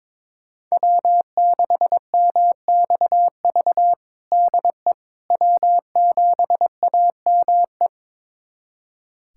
Morse code, heard as a keyed tone, about 22 wpm, 700 Hz. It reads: W6MXV DE W7AME
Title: USA Callsign DE Callsign - 5 Characters - 22wpm